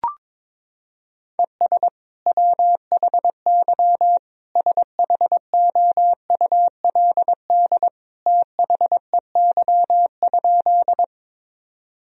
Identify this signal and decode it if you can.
Morse code, heard as a keyed tone, about 22 wpm, 700 Hz. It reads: ES WHY SHOULD THEY?
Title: Sentences from Top 300 Words and Abbreviations 22wpm